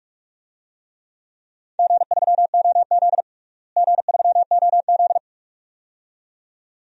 Morse code, heard as a keyed tone, about 41 wpm, 700 Hz. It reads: G3OZ G3OZ
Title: DXCC-100 Call Signs - 4 Characters - Repeated 2x - 45wpm